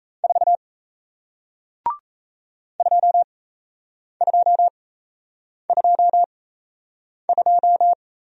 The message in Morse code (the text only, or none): 4 2222